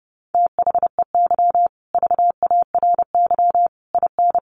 Morse code, heard as a keyed tone, about 28 wpm, 700 Hz. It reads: THEY VARY IN
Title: Sentences from Top 1200 Words 30wpm